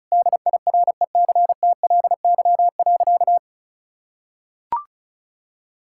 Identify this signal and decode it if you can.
Morse code, heard as a keyed tone, about 34 wpm, 700 Hz. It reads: DIRECTLY.
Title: Sentences from Top 1000 Words - Review Part 5 - 35wpm